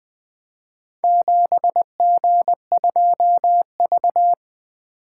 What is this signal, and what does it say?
Morse code, heard as a keyed tone, about 20 wpm, 700 Hz. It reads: 7G2V
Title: Four Letter-Number Combinations - 20wpm